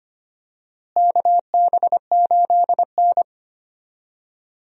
Morse code, heard as a keyed tone, about 25 wpm, 700 Hz. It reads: KB8N
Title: USA Call Signs - 4 Characters - Repeated 3x - 25wpm